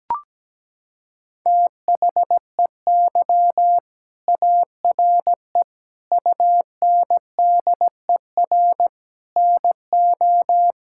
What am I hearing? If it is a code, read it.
Morse code, heard as a keyed tone, about 17 wpm, 700 Hz. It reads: THEY ARE UNDER NO
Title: Sentences from Top 1000 Words - Review Part 6 - 17wpm